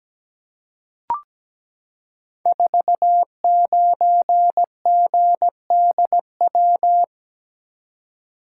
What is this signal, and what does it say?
Morse code, heard as a keyed tone, about 17 wpm, 700 Hz. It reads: 49GDW
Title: Code Groups - 5 Letter:Numbers - 17wpm